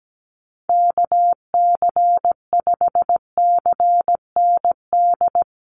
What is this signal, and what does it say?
Morse code, heard as a keyed tone, about 17 wpm, 700 Hz. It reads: KC5CND
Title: USA Call Signs Repeated 2x 17wpm